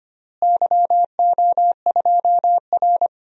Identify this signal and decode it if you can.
Morse code, heard as a keyed tone, about 25 wpm, 700 Hz. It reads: YO2R
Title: DXCC-100 Call Signs - 4 Characters - Repeated 3x - 25wpm